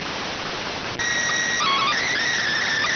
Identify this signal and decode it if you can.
SSTV (slow-scan television), PD 290.